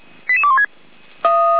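Five-tone selective call (ZVEI), heard as quick runs of five tones, tones of about 70 ms. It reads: 88216
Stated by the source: Five-tone ZVEI selective call 88141, received on SDR.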